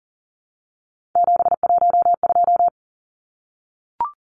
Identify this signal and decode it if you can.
Morse code, heard as a keyed tone, about 40 wpm, 700 Hz. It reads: 712
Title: Numbers - 3-Digit 40wpm